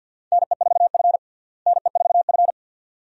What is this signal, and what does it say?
Morse code, heard as a keyed tone, about 53 wpm, 690 Hz. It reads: NE4F NE4F
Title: USA Call Signs - 4 Characters - Repeated 2x - 50wpm